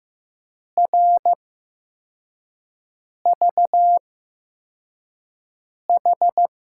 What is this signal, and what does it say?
Morse code, heard as a keyed tone, about 15 wpm, spaced as for 4 wpm, 700 Hz. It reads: RVH
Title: USA Call Signs - 5 Characters - 8x Character Spacing - 15wpm